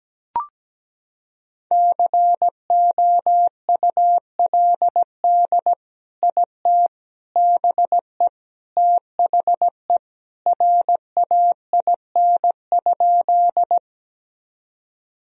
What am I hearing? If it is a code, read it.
Morse code, heard as a keyed tone, about 17 wpm, 700 Hz. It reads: COULD IT BE THE RAIN?
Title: Sentences from Top 1000 Words - Review Part 1 - 17wpm